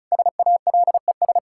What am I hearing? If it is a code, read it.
Morse code, heard as a keyed tone, about 36 wpm, 700 Hz. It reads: SALES
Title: Practice QSO - Exchange 3 - More Info - 35wpm